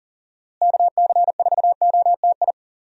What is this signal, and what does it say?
Morse code, heard as a keyed tone, about 40 wpm, 700 Hz. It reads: KC4OTI
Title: USA Call Signs - 6 Characters - Repeated 2x - 40wpm